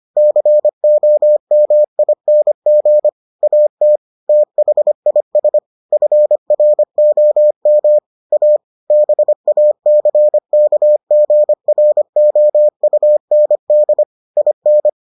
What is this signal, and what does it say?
Morse code, heard as a keyed tone, about 25 wpm, 600 Hz. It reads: COMING AT THIS FROM A BACKGROUND IN